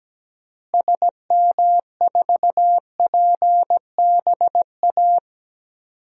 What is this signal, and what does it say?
Morse code, heard as a keyed tone, about 17 wpm, 700 Hz. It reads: SM4PBA
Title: DXCC-100 Call Signs - 6 Characters - Repeated 3x - 17wpm